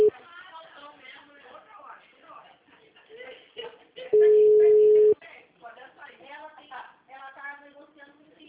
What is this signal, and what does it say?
Ringback tone (Europe, 425 Hz), with speech.